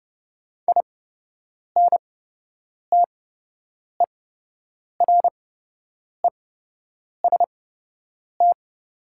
Morse code, heard as a keyed tone, about 30 wpm, spaced as for 8 wpm, 700 Hz. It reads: INTEREST
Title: 8x Character Spacing - Top 500 Words - 30wpm